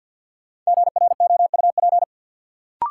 Morse code, heard as a keyed tone, about 53 wpm, 700 Hz. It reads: GROUP
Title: Sentences from Top 1000 Words - Review Part 1 - New Word Drill - 50wpm